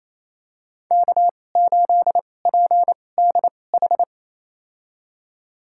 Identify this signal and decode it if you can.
Morse code, heard as a keyed tone, about 28 wpm, spaced as for 20 wpm, 700 Hz. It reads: K8PDH
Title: USA Call Signs - 5 Characters - 2x Character Spacing - 28wpm